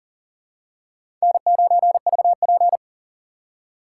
Morse code, heard as a keyed tone, about 40 wpm, 700 Hz. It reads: N9VP
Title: USA Call Signs - 4 Characters - 4x Character Spacing - 40wpm